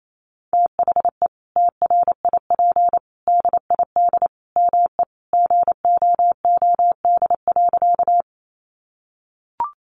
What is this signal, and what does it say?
Morse code, heard as a keyed tone, about 28 wpm, 700 Hz. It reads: THE TRIP DID ME GOOD.
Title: Sentences from Top 1300 Words 28wpm